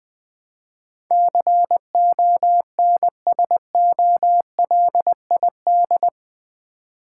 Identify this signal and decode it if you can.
Morse code, heard as a keyed tone, about 19 wpm, 700 Hz. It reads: CONSOLID
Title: Build A Word - 4 Letters At A Time - 11-Letter Words - 20wpm